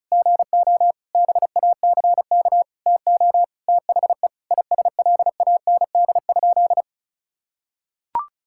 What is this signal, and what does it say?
Morse code, heard as a keyed tone, about 36 wpm, 700 Hz. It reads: GO BACK TO THE ISLAND?
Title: Sentences from Top 1600 Words 35wpm